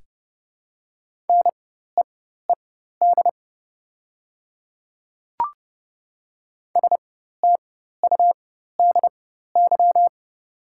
Morse code, heard as a keyed tone, about 30 wpm, spaced as for 14 wpm, 700 Hz. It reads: NEED STUDY
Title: 4x Character Spacing - Top 500 Words - 30wpm